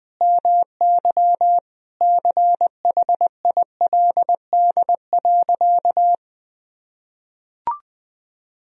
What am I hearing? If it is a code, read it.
Morse code, heard as a keyed tone, about 21 wpm, 700 Hz. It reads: MY CHILD.
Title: Sentences from Top 1000 Words - Review Part 1 - 20wpm